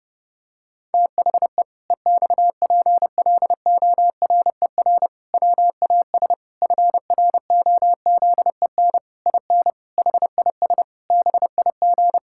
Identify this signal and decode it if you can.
Morse code, heard as a keyed tone, about 30 wpm, 700 Hz. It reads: THE EXPLORER WAS FROZEN IN HIS BIG
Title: Pangram Challenge v2 - 30wpm